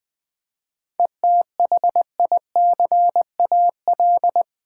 Morse code, heard as a keyed tone, about 20 wpm, 700 Hz. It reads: ETHICAL
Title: 7 Letter Words - First 3 Letters Given - 20wpm